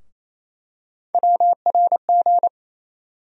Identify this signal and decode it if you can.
Morse code, heard as a keyed tone, about 28 wpm, 700 Hz. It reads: WRG
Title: USA Call Sign Postfixes - 3 Letters 28wpm